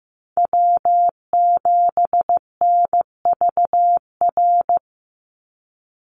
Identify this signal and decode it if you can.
Morse code, heard as a keyed tone, about 15 wpm, 700 Hz. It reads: W7NVR
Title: USA Call Signs 15wpm